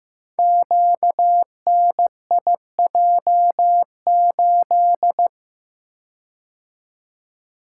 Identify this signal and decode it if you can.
Morse code, heard as a keyed tone, about 15 wpm, 700 Hz. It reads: QNIJ8
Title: Code Groups - 5 Letter:Numbers - 15wpm